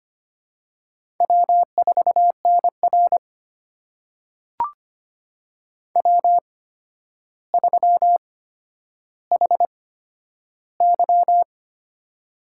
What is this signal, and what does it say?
Morse code, heard as a keyed tone, about 25 wpm, spaced as for 11 wpm, 700 Hz. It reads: W4NR W3HY
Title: USA Call Signs - 4 Characters - 8x Character Spacing - 25wpm